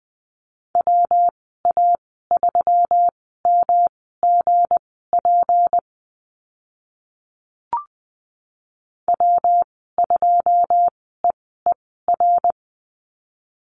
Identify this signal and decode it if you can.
Morse code, heard as a keyed tone, about 20 wpm, spaced as for 14 wpm, 700 Hz. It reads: WA3MGP W2EER
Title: 167 Lesson 27g - Number 7 - Call Signs 20wpm